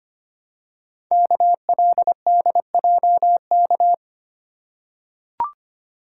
Morse code, heard as a keyed tone, about 25 wpm, 700 Hz. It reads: KLDJK
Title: Code Groups - 5 Letters - 25wpm